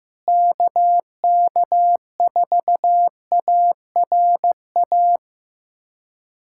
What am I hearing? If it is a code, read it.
Morse code, heard as a keyed tone, about 16 wpm, 700 Hz. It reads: KK4ARA
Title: USA Callsign DE Callsign - 6 Characters - 15wpm